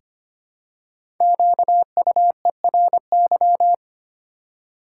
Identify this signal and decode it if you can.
Morse code, heard as a keyed tone, about 25 wpm, 700 Hz. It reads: QUERY
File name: 5 Letter Words - First 3 Letters Given - 25wpm